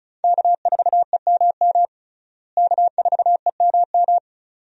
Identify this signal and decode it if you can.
Morse code, heard as a keyed tone, about 36 wpm, 700 Hz. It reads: K4EMM K4EMM
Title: USA Call Signs - 5 Characters - Repeated 2x - 35wpm